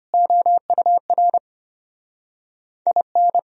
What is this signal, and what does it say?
Morse code, heard as a keyed tone, about 34 wpm, 700 Hz. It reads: OUR IN